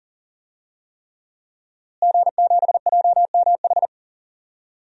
Morse code, heard as a keyed tone, about 40 wpm, 700 Hz. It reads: G7JMH
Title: DXCC-100 Call Signs - 5 Characters - Repeated 3x - 40wpm